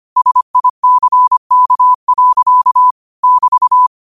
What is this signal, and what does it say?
Morse code, heard as a keyed tone, about 25 wpm, 1000 Hz. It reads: SICK. <BT>
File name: qotdfiles / 25wpm